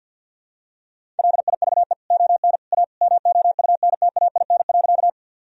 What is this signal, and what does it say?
Morse code, heard as a keyed tone, about 49 wpm, 700 Hz. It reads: LIVE ON A MOUNTAIN.
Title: Sentences from Top 1000 Words - Review Part 1 - 50wpm